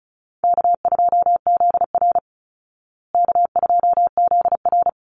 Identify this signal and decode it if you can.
Morse code, heard as a keyed tone, about 34 wpm, 700 Hz. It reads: K2ZR K2ZR
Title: Easy Call Signs Repeated 2x - 35wpm